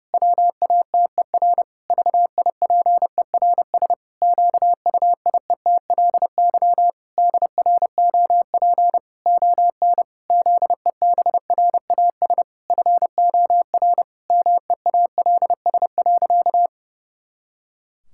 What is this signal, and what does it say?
Morse code, heard as a keyed tone, about 30 wpm, 700 Hz. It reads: WATER VIPERS QUIETLY DROP ON ZEBRAS FOR MEALS.
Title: Pangram Challenge v2 - 30wpm